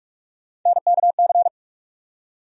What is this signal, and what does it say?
Morse code, heard as a keyed tone, about 47 wpm, 700 Hz. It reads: NKC
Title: USA Call Sign Postfixes - 3 Letters 45wpm